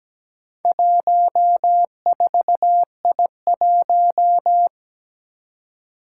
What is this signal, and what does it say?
Morse code, heard as a keyed tone, about 17 wpm, 700 Hz. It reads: 14I1
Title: Four Challenging Letter-Number Combinations 17wpm